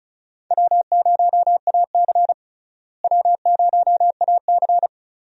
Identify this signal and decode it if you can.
Morse code, heard as a keyed tone, about 36 wpm, 700 Hz. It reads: W0AC W0AC
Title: USA Call Signs - 4 Characters - Repeated 2x - 35wpm